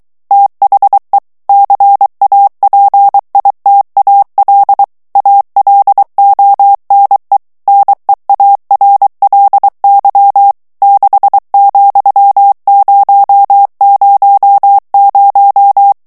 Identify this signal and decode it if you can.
Morse code, heard as a keyed tone, about 23 wpm, 800 Hz. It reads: THE CAPITAL ALONE NEARLY 6,000
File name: morse mp3 / ANGLAIS texte clair 2 vitesse 22 mots minute